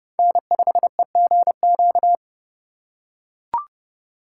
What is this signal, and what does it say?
Morse code, heard as a keyed tone, about 27 wpm, 700 Hz. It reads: N5EGQ